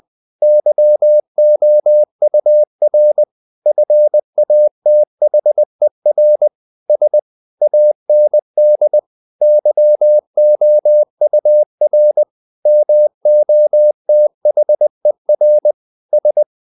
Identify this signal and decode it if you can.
Morse code, heard as a keyed tone, about 20 wpm, 600 Hz. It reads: YOUR FATHER S AND YOUR MOTHER S
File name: the old man and the sea / the old man and the sea part 1 20wpm